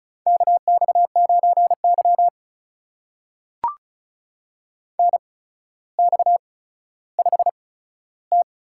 Morse code, heard as a keyed tone, about 36 wpm, spaced as for 15 wpm, 700 Hz. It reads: KX9Y NX5T
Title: USA Call Signs - 4 Characters - 8x Character Spacing - 35wpm